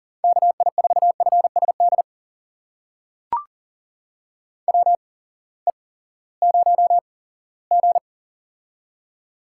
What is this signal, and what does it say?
Morse code, heard as a keyed tone, about 40 wpm, spaced as for 20 wpm, 700 Hz. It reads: KI4FSD WE0G
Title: USA Call Signs - 8x Character Spacing - 40wpm